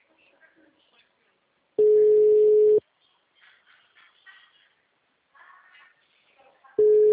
Ringback tone (Europe, 425 Hz).